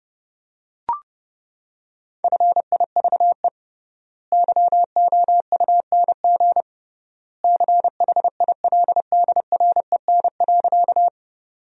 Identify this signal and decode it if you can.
Morse code, heard as a keyed tone, about 30 wpm, 700 Hz. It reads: FIVE YOUNG CHILDREN.
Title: 3x Word-Spacing - Sentences from Top 400 Words - 30wpm